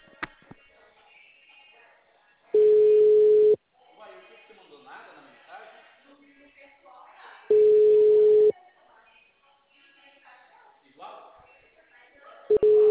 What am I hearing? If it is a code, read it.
Ringback tone (Europe, 425 Hz), with speech.